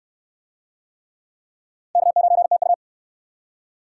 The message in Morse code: K9TU

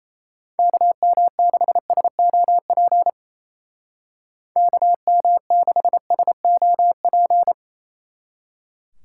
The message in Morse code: KM6SOP KM6SOP

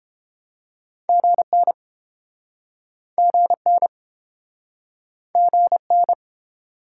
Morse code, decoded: GN GN GN

